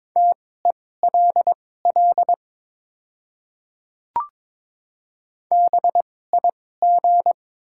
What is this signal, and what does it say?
Morse code, heard as a keyed tone, about 22 wpm, spaced as for 16 wpm, 700 Hz. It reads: TELL BIG
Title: 2x Character Spacing - Top 200 Words - 22wpm